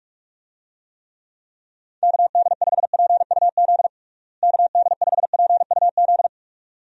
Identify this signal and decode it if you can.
Morse code, heard as a keyed tone, about 46 wpm, 700 Hz. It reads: KD5PUZ KD5PUZ
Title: USA Call Signs - 6 Characters - Repeated 2x - 45wpm